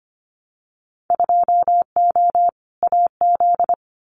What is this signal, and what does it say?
Morse code, heard as a keyed tone, about 25 wpm, 700 Hz. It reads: 2O AZ
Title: Winter Field Day Exchanges 25wpm